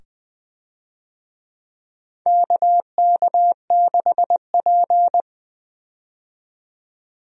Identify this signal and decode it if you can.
Morse code, heard as a keyed tone, about 21 wpm, 700 Hz. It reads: KK6P
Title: USA Call Signs - 4 Characters - 20wpm